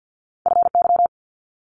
Morse code, heard as a keyed tone, about 50 wpm, 700 Hz. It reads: FY
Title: Multi-letter Phonemes 50wpm